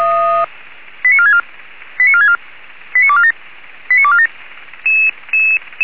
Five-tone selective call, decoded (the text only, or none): 89464 89464 89247 89247